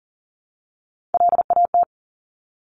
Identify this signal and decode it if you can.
Morse code, heard as a keyed tone, about 38 wpm, 700 Hz. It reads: LAT
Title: Top Three Letter Combinations in Words 40wpm